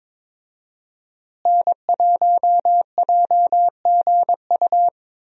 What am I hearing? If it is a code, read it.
Morse code, heard as a keyed tone, about 22 wpm, 700 Hz. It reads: N1JGU
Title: USA Call Signs - 5 Characters - Repeated 2x - 22wpm